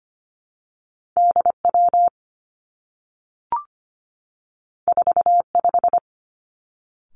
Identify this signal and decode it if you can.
Morse code, heard as a keyed tone, about 25 wpm, 700 Hz. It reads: D W 4 5
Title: Two Challenging Letter-Number Combinations 25wpm